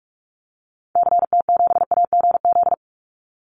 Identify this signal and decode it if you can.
Morse code, heard as a keyed tone, about 44 wpm, 700 Hz. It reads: CT7AGZ
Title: Call Signs - 45wpm